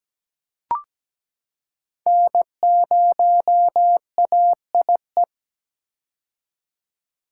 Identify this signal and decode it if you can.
Morse code, heard as a keyed tone, about 17 wpm, 700 Hz. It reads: N0AIE
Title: K1USN SST QSO Practice - 17wpm